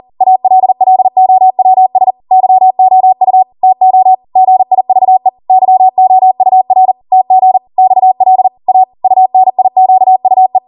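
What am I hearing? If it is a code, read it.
Morse code, heard as a keyed tone, about 40 wpm, 750 Hz. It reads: ALLOWS YOU TO GIVE YOUR TG XL A UNIQUE
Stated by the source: Code practice at 40 wpm.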